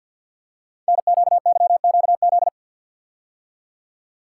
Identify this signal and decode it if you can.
Morse code, heard as a keyed tone, about 48 wpm, 700 Hz. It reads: NQYQZ
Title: Code Groups - 5 Letters - 50wpm